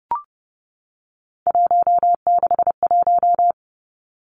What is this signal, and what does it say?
Morse code, heard as a keyed tone, about 31 wpm, 700 Hz. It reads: 161
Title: Three Challenging Letter-Number Combinations 30wpm